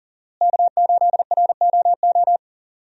Morse code, heard as a keyed tone, about 40 wpm, 700 Hz. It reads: K8ROO
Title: USA Call Signs - 5 Characters - Repeated 3x - 40wpm